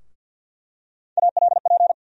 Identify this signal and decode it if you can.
Morse code, heard as a keyed tone, about 49 wpm, 700 Hz. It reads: ALP
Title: USA Call Sign Postfixes - 3 Letters 50wpm